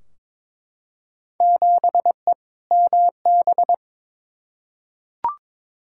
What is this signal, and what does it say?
Morse code, heard as a keyed tone, about 21 wpm, 700 Hz. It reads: 7E MB
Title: Field Day Exchanges Encore - 22wpm